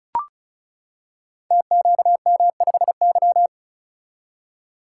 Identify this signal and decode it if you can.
Morse code, heard as a keyed tone, about 36 wpm, 700 Hz. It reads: TQM5Y
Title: Code Groups - 5 Letter:Numbers - 35wpm